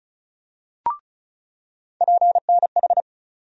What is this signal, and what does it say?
Morse code, heard as a keyed tone, about 34 wpm, 700 Hz. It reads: PNH
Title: USA Call Sign Postfixes - 3 Letters 35wpm